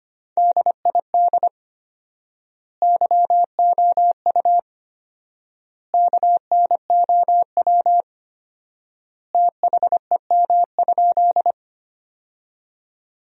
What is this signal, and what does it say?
Morse code, heard as a keyed tone, about 25 wpm, 700 Hz. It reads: DID YOU KNOW THEM?
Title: 4x Word-Spacing - Sentences from Top 100 Words - 25wpm